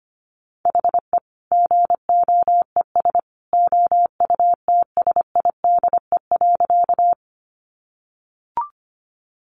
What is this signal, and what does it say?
Morse code, heard as a keyed tone, about 25 wpm, 700 Hz. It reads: HE GOES OUTSIDE.
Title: Additional Easy Phrases 25wpm